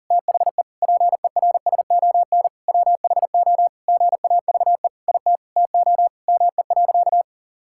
Morse code, heard as a keyed tone, about 40 wpm, 700 Hz. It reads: THE PERSON WHO GAVE IT TO ME.
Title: Sentences from Top 1600 Words 40wpm